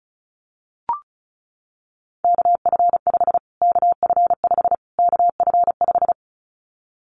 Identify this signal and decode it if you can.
Morse code, heard as a keyed tone, about 34 wpm, 700 Hz. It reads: KF5 KF5 KF5
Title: Callsign Prefixes for Top 25 DXCC - Repeated 3x - 35wpm